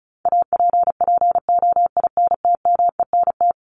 Morse code, heard as a keyed tone, about 34 wpm, 700 Hz. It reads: APPOINTMENT
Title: Top 1751-2000 Words 35wpm